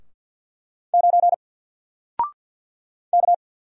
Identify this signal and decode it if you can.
Morse code, heard as a keyed tone, about 48 wpm, 700 Hz. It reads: 9 K
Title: Single Letter-Number 50wpm